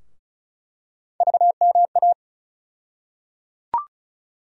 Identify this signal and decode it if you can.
Morse code, heard as a keyed tone, about 34 wpm, 700 Hz. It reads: VMA
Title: USA Call Sign Postfixes - 3 Letters 35wpm